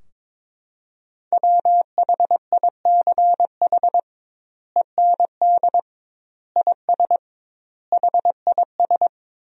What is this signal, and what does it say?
Morse code, heard as a keyed tone, about 22 wpm, 700 Hz. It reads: WHICH END IS HIS
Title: 2x Word-Spacing - Sentences from Top 300 Words - 22wpm